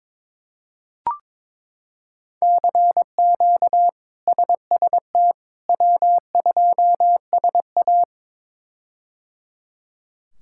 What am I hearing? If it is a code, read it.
Morse code, heard as a keyed tone, about 21 wpm, 700 Hz. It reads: CQ SST W2SA
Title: K1USN SST QSO Practice - 22wpm